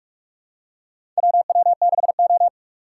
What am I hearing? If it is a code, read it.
Morse code, heard as a keyed tone, about 44 wpm, 700 Hz. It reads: WW6O
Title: USA Call Signs - 4 Characters - 6x Character Spacing - 45wpm